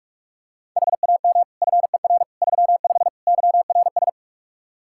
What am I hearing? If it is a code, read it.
Morse code, heard as a keyed tone, about 43 wpm, 700 Hz. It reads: HAM FER 35 YRS
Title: K1USN SST QSO Practice 45wpm